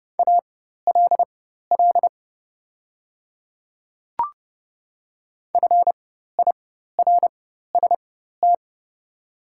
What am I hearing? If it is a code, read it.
Morse code, heard as a keyed tone, about 30 wpm, spaced as for 14 wpm, 700 Hz. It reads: ALL FIRST
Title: 4x Character Spacing - Top 300 Words - 30wpm